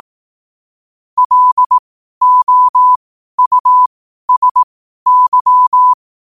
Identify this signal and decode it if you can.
Morse code, heard as a keyed tone, about 18 wpm, spaced as for 12 wpm, 1000 Hz. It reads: LOUSY
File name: qotdfiles / 12wpm